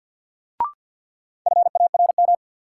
Morse code, heard as a keyed tone, about 50 wpm, 700 Hz. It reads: FARM
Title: Sentences from Top 1200 Words - New Word Drill Alt - 50wpm